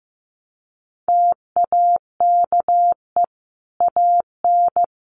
Morse code, heard as a keyed tone, about 15 wpm, 700 Hz. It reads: TAKE AN
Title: Sentences from Top 1300 Words 15wpm